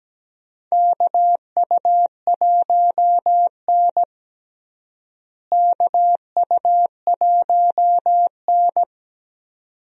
Morse code, heard as a keyed tone, about 17 wpm, 700 Hz. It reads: KU1N KU1N
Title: USA Call Signs - 4 Characters - Repeated 2x - 17wpm